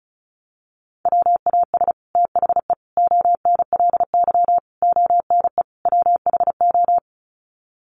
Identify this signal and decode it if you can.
Morse code, heard as a keyed tone, about 34 wpm, 700 Hz. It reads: WAS THE ONLY ONE WHO
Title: Sets of 5 Words Encore2 35wpm